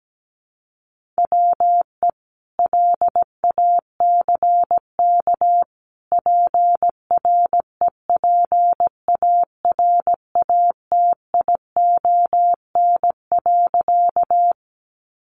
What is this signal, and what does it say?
Morse code, heard as a keyed tone, about 17 wpm, 700 Hz. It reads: WE LACK PREPARATION.
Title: Sentences from Top 1400 Words 17wpm